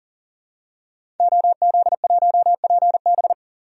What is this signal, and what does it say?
Morse code, heard as a keyed tone, about 40 wpm, 700 Hz. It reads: OZ1PB
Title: DXCC-100 Call Signs - 5 Characters - 40wpm